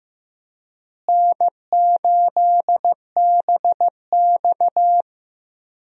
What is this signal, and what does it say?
Morse code, heard as a keyed tone, about 15 wpm, 700 Hz. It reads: N8BX
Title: USA Call Signs - 4 Characters - Repeated 3x - 15wpm